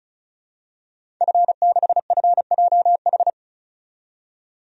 Morse code, heard as a keyed tone, about 36 wpm, 700 Hz. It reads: F6FJH
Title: DXCC-100 Call Signs - 5 Characters - 35wpm